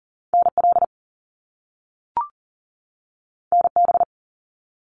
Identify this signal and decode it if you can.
Morse code, heard as a keyed tone, about 41 wpm, 700 Hz. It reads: NL NB
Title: Canadian Provinces and Territories 40wpm